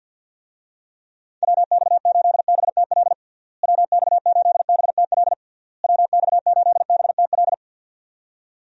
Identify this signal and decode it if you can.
Morse code, heard as a keyed tone, about 49 wpm, 700 Hz. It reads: WX8BTL WX8BTL WX8BTL
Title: USA Call Signs - 6 Characters - Repeated 3x - 50wpm